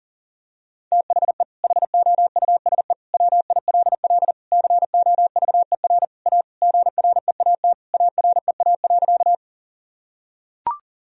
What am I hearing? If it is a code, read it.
Morse code, heard as a keyed tone, about 37 wpm, 700 Hz. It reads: THE HOUSE WILL COVER A GREAT AREA.
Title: Sentences from Top 1000 Words - Review Part 2 - 40wpm